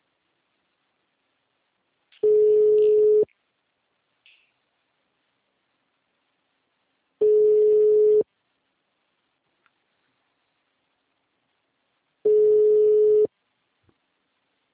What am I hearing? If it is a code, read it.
Ringback tone (Europe, 425 Hz), with speech.
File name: out- -2001- - - .13